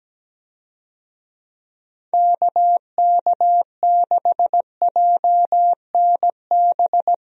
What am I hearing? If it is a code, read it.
Morse code, heard as a keyed tone, about 17 wpm, 700 Hz. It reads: KK6JNB
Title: USA Call Signs - 6 Characters - Repeated 3x - 17wpm